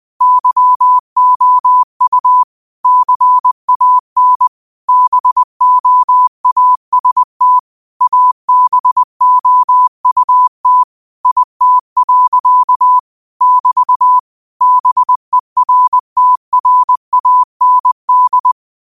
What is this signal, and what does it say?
Morse code, heard as a keyed tone, about 20 wpm, 1000 Hz. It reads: YOU CAN BOAST ABOUT IT. <BT> BERTRAND
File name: qotdfiles / 20wpm